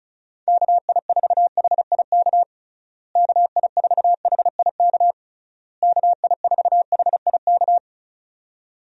Morse code, heard as a keyed tone, about 34 wpm, 700 Hz. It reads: KI4HIK KI4HIK KI4HIK
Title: USA Call Signs - 6 Characters - Repeated 3x - 35wpm